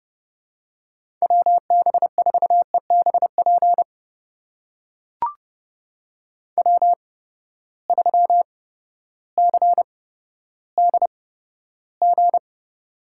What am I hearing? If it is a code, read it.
Morse code, heard as a keyed tone, about 30 wpm, spaced as for 14 wpm, 700 Hz. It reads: WB4EBP W3CDG
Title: USA Call Signs - 8x Character Spacing - 30wpm